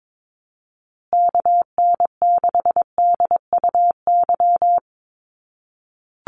Morse code, heard as a keyed tone, about 22 wpm, 700 Hz. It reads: KN6DUY
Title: USA Call Signs Repeated 3x 22wpm